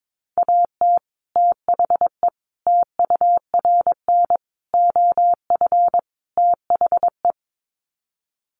Morse code, heard as a keyed tone, about 22 wpm, 700 Hz. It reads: AT THE TURN OF THE
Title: Sets of 5 Words 22wpm